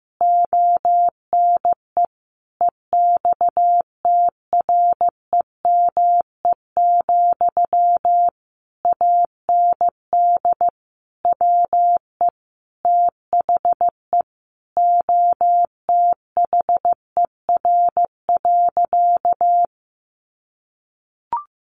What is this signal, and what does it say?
Morse code, heard as a keyed tone, about 14 wpm, 700 Hz. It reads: ONE EXTREME, AND WE THE OTHER.
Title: Sentences from Top 1300 Words 15wpm